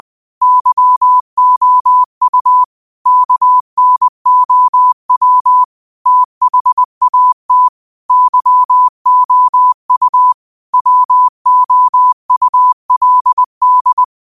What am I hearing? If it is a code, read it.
Morse code, heard as a keyed tone, about 20 wpm, 1000 Hz. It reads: YOU KNOW THAT YOU WOULD
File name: qotdfiles / 20wpm